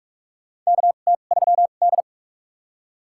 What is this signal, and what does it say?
Morse code, heard as a keyed tone, about 43 wpm, spaced as for 32 wpm, 700 Hz. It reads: KT3D
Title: USA Call Signs - 4 Characters - 2x Character Spacing - 45wpm